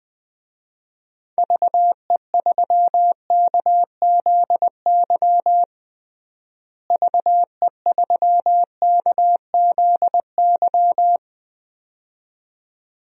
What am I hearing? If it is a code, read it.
Morse code, heard as a keyed tone, about 21 wpm, 700 Hz. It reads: VE3KZY VE3KZY
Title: DXCC-100 Call Signs - 6 Characters - Repeated 2x - 20wpm